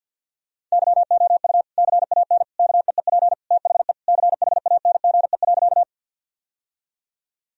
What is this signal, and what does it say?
Morse code, heard as a keyed tone, about 51 wpm, 700 Hz. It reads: YOU CAN KEEP THE CHANGE.
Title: Sentences from Top 1000 Words - Review Part 1 - 50wpm